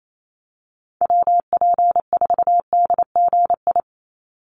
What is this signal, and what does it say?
Morse code, heard as a keyed tone, about 28 wpm, 700 Hz. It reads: WP4DGI
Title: USA Call Signs 28wpm